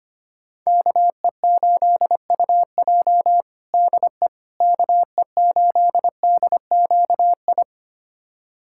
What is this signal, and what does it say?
Morse code, heard as a keyed tone, about 25 wpm, 700 Hz. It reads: KE8UJ DE KE8DQI
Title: USA Callsign DE Callsign 25wpm